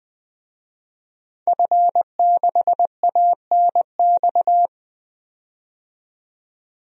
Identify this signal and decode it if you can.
Morse code, heard as a keyed tone, about 20 wpm, 700 Hz. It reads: F6ANX
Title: DXCC-100 Call Signs - 5 Characters - 20wpm